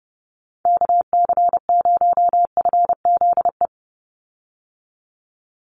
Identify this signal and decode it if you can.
Morse code, heard as a keyed tone, about 30 wpm, 700 Hz. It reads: KC0FZE
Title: USA Call Signs 30wpm